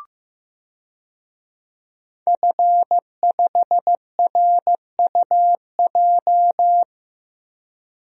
Morse code, heard as a keyed tone, about 15 wpm, 700 Hz. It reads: F5RUJ
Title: DXCC-100 Call Signs - 5 Characters - Repeated 3x - 15wpm